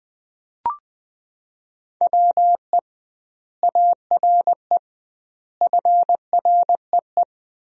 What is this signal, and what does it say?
Morse code, heard as a keyed tone, about 20 wpm, 700 Hz. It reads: WE ARE FREE
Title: 2x Word-Spacing - Sentences from Top 500 Words - 20wpm